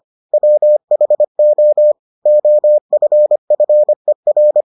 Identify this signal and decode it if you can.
Morse code, heard as a keyed tone, about 25 wpm, 600 Hz. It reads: WHO OFFER